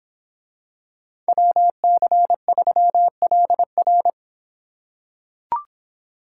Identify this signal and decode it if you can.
Morse code, heard as a keyed tone, about 25 wpm, 700 Hz. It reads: WC3LR